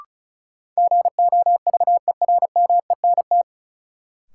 Morse code, heard as a keyed tone, about 36 wpm, 700 Hz. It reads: GOVERMENT
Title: Sentences from Top 1300 Words - New Word Drill Alt - 35wpm